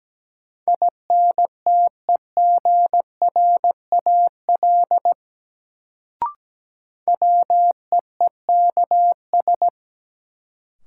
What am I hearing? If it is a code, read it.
Morse code, heard as a keyed tone, about 17 wpm, 700 Hz. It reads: INTEGRAL WEEKS
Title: Sets of 5 Words - New Word Drill - 17wpm